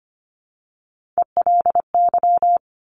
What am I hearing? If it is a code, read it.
Morse code, heard as a keyed tone, about 25 wpm, 700 Hz. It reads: ELY
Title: Top Three Letter Combinations in Words 25wpm